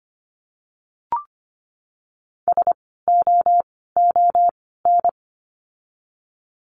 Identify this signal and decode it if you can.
Morse code, heard as a keyed tone, about 25 wpm, spaced as for 16 wpm, 700 Hz. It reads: SOON
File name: course / 25:10wpm / 094 Lesson 16f - Letter c - Words 25wpm